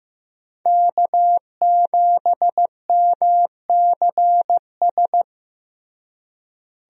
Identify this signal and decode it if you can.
Morse code, heard as a keyed tone, about 15 wpm, 700 Hz. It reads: K7MCS
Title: USA Call Signs - 5 Characters - Repeated 3x - 15wpm